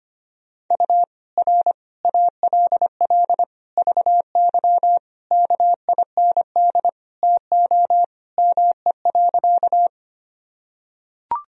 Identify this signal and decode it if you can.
Morse code, heard as a keyed tone, about 25 wpm, 700 Hz. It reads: U R ALL VY KIND TO ME.
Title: Sentences from Top 200 Words and Abbreviations 25wpm